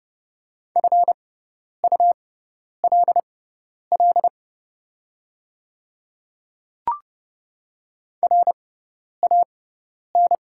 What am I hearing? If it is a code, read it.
Morse code, heard as a keyed tone, about 30 wpm, spaced as for 10 wpm, 700 Hz. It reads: FULL RAN
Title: 6x Character Spacing - Top 500 Words - 30wpm